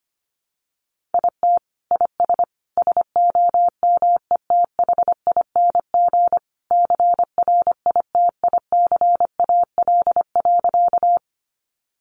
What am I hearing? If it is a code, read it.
Morse code, heard as a keyed tone, about 25 wpm, 700 Hz. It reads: IT IS SOMETHING CRITICAL.
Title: Sentences from Top 1100 Words 25wpm